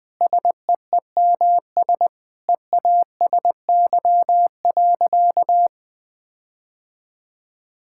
Morse code, heard as a keyed tone, about 20 wpm, 700 Hz. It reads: SEEMS EASY.
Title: Sentences with 4-Word Phrases - 20wpm